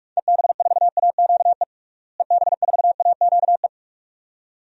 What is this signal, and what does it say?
Morse code, heard as a keyed tone, about 43 wpm, 700 Hz. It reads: EB4AQE EB4AQE
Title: DXCC-100 Call Signs - 6 Characters - Repeated 2x - 45wpm